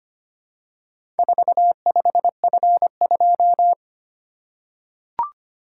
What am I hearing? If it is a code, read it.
Morse code, heard as a keyed tone, about 25 wpm, 700 Hz. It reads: 45F2
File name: practice / 25wpm / Four Letter-Number Combinations - 25wpm